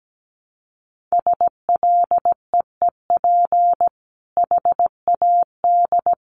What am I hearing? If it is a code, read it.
Morse code, heard as a keyed tone, about 17 wpm, 700 Hz. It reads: SLEEP HAD
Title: Sentences from Top 1100 Words 17wpm